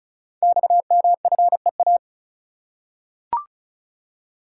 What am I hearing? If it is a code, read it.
Morse code, heard as a keyed tone, about 36 wpm, 700 Hz. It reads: XMFEA
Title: Code Groups - 5 Letter:Numbers - 35wpm